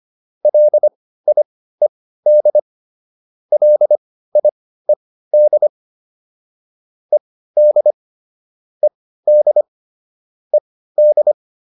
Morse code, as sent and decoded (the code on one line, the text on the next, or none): .-.. .. -.. / .-.. .. -.. / . -.. / . -.. / . -..
LIED LIED ED ED ED